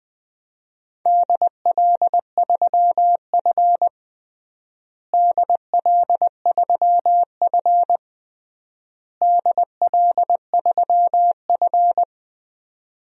Morse code, heard as a keyed tone, about 20 wpm, 700 Hz. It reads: DL3F DL3F DL3F
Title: DXCC-100 Call Signs - 4 Characters - Repeated 3x - 20wpm